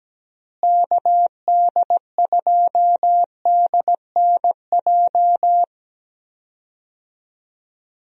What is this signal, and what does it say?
Morse code, heard as a keyed tone, about 17 wpm, 700 Hz. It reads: KD2DNJ
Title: USA Call Signs - 6 Characters - Repeated 2x - 17wpm